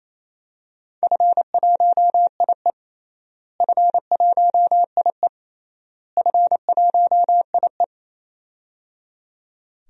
Morse code, heard as a keyed tone, about 28 wpm, 700 Hz. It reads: F1IE F1IE F1IE
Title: DXCC-100 Call Signs - 4 Characters - Repeated 3x - 28wpm